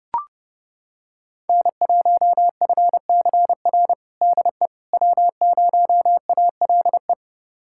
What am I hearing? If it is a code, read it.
Morse code, heard as a keyed tone, about 30 wpm, 700 Hz. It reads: N1FCR DE W0ALE
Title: USA Callsign DE Callsign - 5 Characters - 30wpm